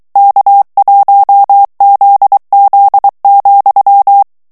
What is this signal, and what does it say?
Morse code, heard as a keyed tone, about 23 wpm, 800 Hz. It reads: K1ZZ,
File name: ANGLAIS texte clair 3 vitesse 22 mots minute